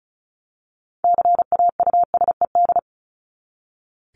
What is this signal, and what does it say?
Morse code, heard as a keyed tone, about 34 wpm, 700 Hz. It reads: CAUSED
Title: ED Suffix - Top 100 Words with Base Words 35wpm